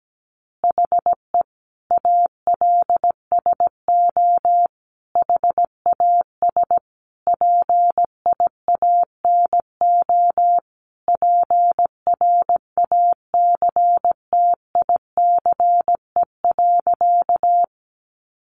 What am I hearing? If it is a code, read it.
Morse code, heard as a keyed tone, about 17 wpm, 700 Hz. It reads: HE ALSO HAS PIANO PRACTICE.
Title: Additional Easy Phrases 17wpm